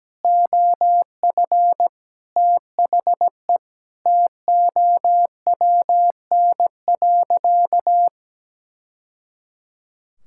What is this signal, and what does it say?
Morse code, heard as a keyed tone, about 17 wpm, 700 Hz. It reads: OF THE TOWN.
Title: Sentences from Top 1000 Words - Review Part 5 - 17wpm